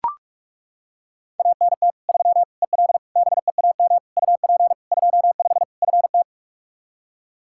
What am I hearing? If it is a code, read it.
Morse code, heard as a keyed tone, about 41 wpm, 700 Hz. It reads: ANT 3 EL BEAM UP 25 FT
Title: Practice QSO - Exchange 2 - Info - 45wpm